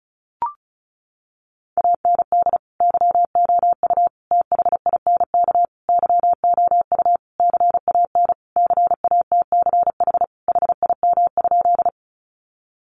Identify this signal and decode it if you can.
Morse code, heard as a keyed tone, about 34 wpm, 700 Hz. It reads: AND YOU THINK YOU CAN CATCH HIM?
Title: Sentences from Top 1500 Words 35wpm